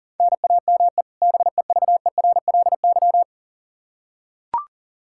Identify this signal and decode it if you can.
Morse code, heard as a keyed tone, about 37 wpm, 700 Hz. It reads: NAME BEVERLY
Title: Practice QSO - Exchange 1 - Beginning - 40wpm